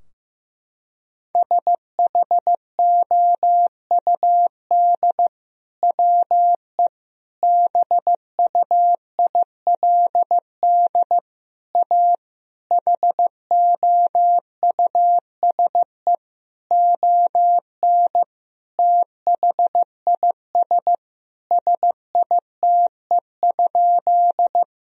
Morse code, heard as a keyed tone, about 15 wpm, 700 Hz. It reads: SHOUD WE BUILD A HOUSE ON THIS SITE?
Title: Sentences from Top 1000 Words - Review Part 4 - 15wpm